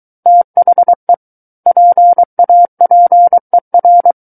Morse code, heard as a keyed tone, about 23 wpm, 700 Hz. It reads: THE PAPER
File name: Book One / 14-Chapter 14 - 23 WPM